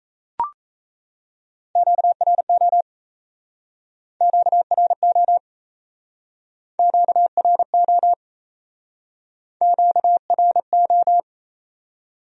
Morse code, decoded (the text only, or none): QRO QRO QRO QRO